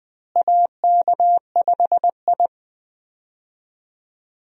Morse code, heard as a keyed tone, about 20 wpm, 700 Hz. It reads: AK5I
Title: USA Call Signs - 4 Characters - Repeated 3x - 20wpm